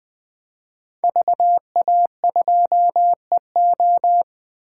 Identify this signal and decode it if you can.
Morse code, heard as a keyed tone, about 20 wpm, 700 Hz. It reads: VA2EO
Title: 2023 SOTA Chaser Call Signs - 20wpm